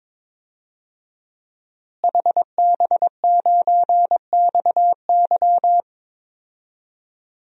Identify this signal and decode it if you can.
Morse code, heard as a keyed tone, about 22 wpm, 700 Hz. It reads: HB9XY
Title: DXCC-100 Call Signs - 5 Characters - Repeated 3x - 22wpm